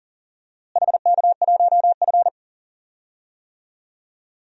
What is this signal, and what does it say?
Morse code, heard as a keyed tone, about 40 wpm, 700 Hz. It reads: HK1F
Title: DXCC-100 Call Signs - 4 Characters - Repeated 3x - 40wpm